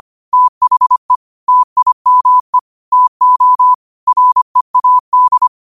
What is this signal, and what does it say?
Morse code, heard as a keyed tone, about 25 wpm, 1000 Hz. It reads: THE TIME TO READ
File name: qotdfiles / 25wpm